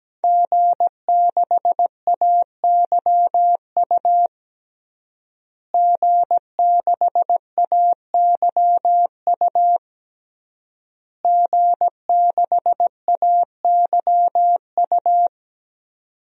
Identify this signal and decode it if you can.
Morse code, heard as a keyed tone, about 17 wpm, 700 Hz. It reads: G6AYU G6AYU G6AYU
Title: DXCC-100 Call Signs - 5 Characters - Repeated 3x - 17wpm